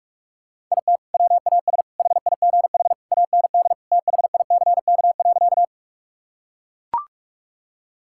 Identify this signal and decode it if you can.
Morse code, heard as a keyed tone, about 41 wpm, 700 Hz. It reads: IT WAS HIGH AND THICK.
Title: Sentences from Top 1600 Words 45wpm